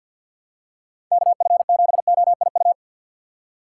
Morse code, heard as a keyed tone, about 53 wpm, 700 Hz. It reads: KF7OIU
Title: USA Call Signs - 2x Character Spacing - 50wpm